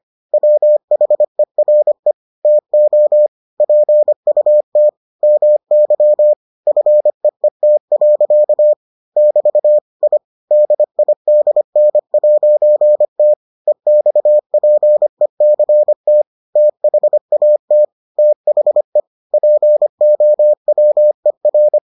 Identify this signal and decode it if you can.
Morse code, heard as a keyed tone, about 25 wpm, 600 Hz. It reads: WHERE TO PUT MY FEET. <BT> I DIDN'T EXPECT THAT THE POWER